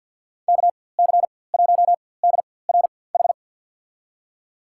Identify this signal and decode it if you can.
Morse code, heard as a keyed tone, about 53 wpm, spaced as for 24 wpm, 700 Hz. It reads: KC1DRH
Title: USA Call Signs - 6 Characters - 4x Character Spacing - 50wpm